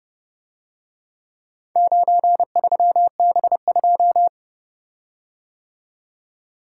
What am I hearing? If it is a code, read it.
Morse code, heard as a keyed tone, about 30 wpm, 700 Hz. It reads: 93B2